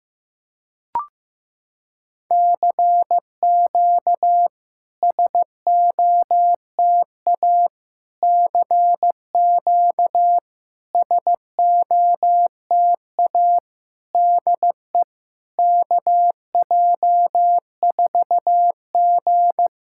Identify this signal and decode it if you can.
Morse code, heard as a keyed tone, about 15 wpm, 700 Hz. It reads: CQ SOTA CQ SOTA DE KJ4G
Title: SOTA QSO Practice 15wpm